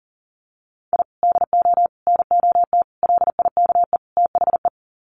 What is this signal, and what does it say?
Morse code, heard as a keyed tone, about 40 wpm, 700 Hz. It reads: I DO NOT LIKE THE
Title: Sets of 5 Words Encore 40wpm